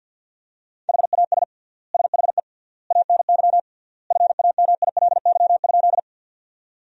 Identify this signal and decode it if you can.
Morse code, heard as a keyed tone, about 46 wpm, 700 Hz. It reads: HAS SHE ANY FAMILY?
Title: 3x Word-Spacing - Sentences from Top 400 Words - 50wpm